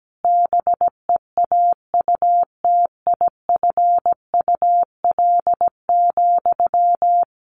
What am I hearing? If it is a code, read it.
Morse code, heard as a keyed tone, about 17 wpm, 700 Hz. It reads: BEAUTIFUL,
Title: Sentences from Top 1200 Words 17wpm